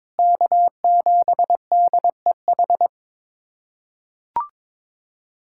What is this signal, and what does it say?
Morse code, heard as a keyed tone, about 22 wpm, 700 Hz. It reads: K7DEH
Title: USA Callsign DE Callsign - 5 Characters - 22wpm